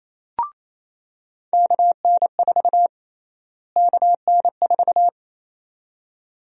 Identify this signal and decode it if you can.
Morse code, heard as a keyed tone, about 28 wpm, 700 Hz. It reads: KN4 KN4
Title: USA Call Sign Prefixes - Repeated 2x - 28wpm